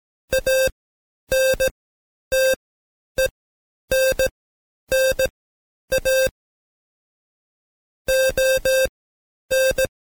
Morse code, heard as a keyed tone, about 17 wpm, spaced as for 10 wpm, 530 Hz. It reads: ANTENNA ON
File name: Code Quick / AudioDownld2 / Seg131-Stepper4 9WPM